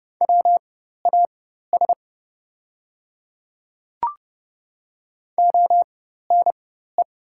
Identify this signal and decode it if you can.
Morse code, heard as a keyed tone, about 29 wpm, spaced as for 14 wpm, 700 Hz. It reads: WAS ONE
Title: 4x Character Spacing - Top 100 Words - 30wpm